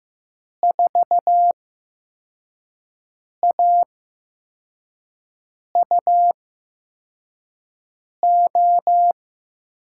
Morse code, heard as a keyed tone, about 15 wpm, spaced as for 4 wpm, 700 Hz. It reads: 4AUO
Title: USA Call Signs - 5 Characters - 8x Character Spacing - 15wpm